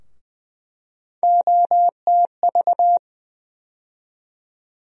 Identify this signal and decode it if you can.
Morse code, heard as a keyed tone, about 20 wpm, 700 Hz. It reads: OTV
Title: USA Call Sign Postfixes - 3 Letters 20wpm